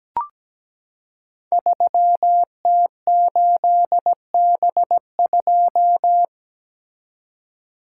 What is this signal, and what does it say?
Morse code, heard as a keyed tone, about 17 wpm, 700 Hz. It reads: 3T8B2
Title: Code Groups - 5 Letter:Numbers - 17wpm